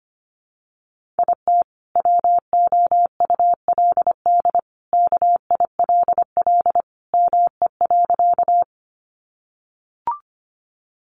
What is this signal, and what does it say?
Morse code, heard as a keyed tone, about 25 wpm, 700 Hz. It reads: IT WOULD KILL ME.
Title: Sentences from Top 1400 Words 25wpm